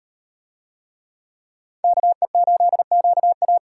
Morse code, heard as a keyed tone, about 38 wpm, 700 Hz. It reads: KE8QA